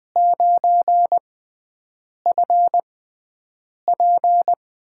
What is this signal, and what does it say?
Morse code, heard as a keyed tone, about 20 wpm, spaced as for 7 wpm, 700 Hz. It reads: 9FP
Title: USA Call Signs - 6 Characters - 6x Character Spacing - 20wpm